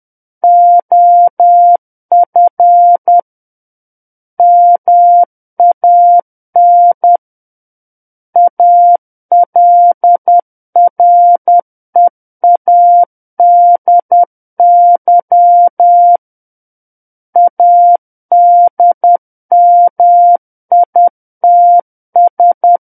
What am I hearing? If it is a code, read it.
Morse code, heard as a keyed tone, about 10 wpm, 700 Hz. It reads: OF MAN ALREADY ADMITS
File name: Book One / 01-Chapter 1 - 10 WPM